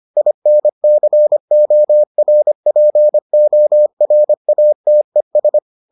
Morse code, heard as a keyed tone, about 25 wpm, 600 Hz. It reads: INCORPORATES